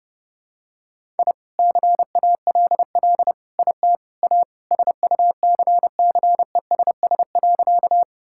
Morse code, heard as a keyed tone, about 32 wpm, 700 Hz. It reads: I CALL IT A SUCCESS.
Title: Sentences from Top 1000 Words - Review Part 6 - 30wpm